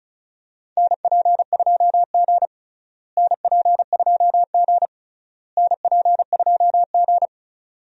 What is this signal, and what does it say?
Morse code, heard as a keyed tone, about 34 wpm, 700 Hz. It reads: NP2G NP2G NP2G
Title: USA Call Signs - 4 Characters - Repeated 3x - 35wpm